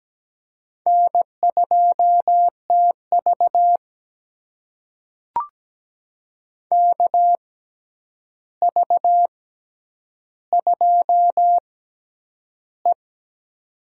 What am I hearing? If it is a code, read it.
Morse code, heard as a keyed tone, about 17 wpm, spaced as for 9 wpm, 700 Hz. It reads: N2TV KV2E
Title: USA Call Signs - 4 Characters - 6x Character Spacing - 17wpm